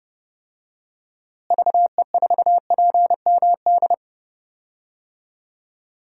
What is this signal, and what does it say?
Morse code, heard as a keyed tone, about 30 wpm, 700 Hz. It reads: VE4PMD
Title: DXCC-100 Call Signs - 6 Characters - Repeated 3x - 30wpm